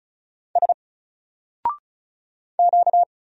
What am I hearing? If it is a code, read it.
Morse code, heard as a keyed tone, about 34 wpm, 700 Hz. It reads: S Q
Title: Single Letter-Number 35wpm